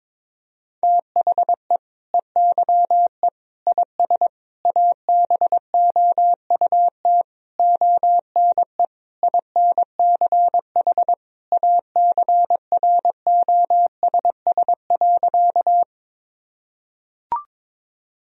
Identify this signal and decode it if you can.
Morse code, heard as a keyed tone, about 22 wpm, 700 Hz. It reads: THE EYE IS ABOUT ONE INCH ACROSS.
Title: Sentences from Top 1000 Words - Review Part 3 - 22wpm